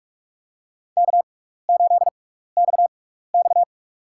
Morse code, heard as a keyed tone, about 46 wpm, spaced as for 16 wpm, 700 Hz. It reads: K8XX
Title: USA Call Signs - 4 Characters - 6x Character Spacing - 45wpm